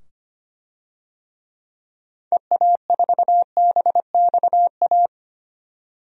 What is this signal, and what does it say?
Morse code, heard as a keyed tone, about 25 wpm, 700 Hz. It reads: EA4BXA
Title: DXCC-100 Call Signs - 6 Characters - 25wpm